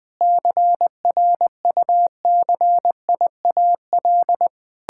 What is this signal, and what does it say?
Morse code, heard as a keyed tone, about 20 wpm, 700 Hz. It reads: CRUCIAL
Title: Sentences with 9-Word Phrases - Spoken Beginning - 20wpm